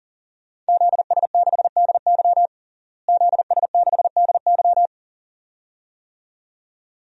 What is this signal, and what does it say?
Morse code, heard as a keyed tone, about 40 wpm, 700 Hz. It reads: ZS6DY ZS6DY
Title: DXCC-100 Call Signs - 5 Characters - Repeated 2x - 40wpm